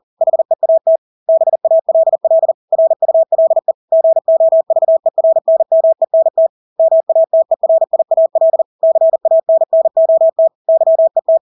Morse code, heard as a keyed tone, about 40 wpm, 650 Hz. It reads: HEAT BALL RULE GOVERNMENT MATERIAL CANNOT YET
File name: Engelske ord 40-40 wpm